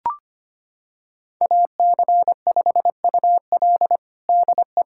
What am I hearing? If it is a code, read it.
Morse code, heard as a keyed tone, about 25 wpm, 700 Hz. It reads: AC5UL DE
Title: USA Callsign DE Callsign 25wpm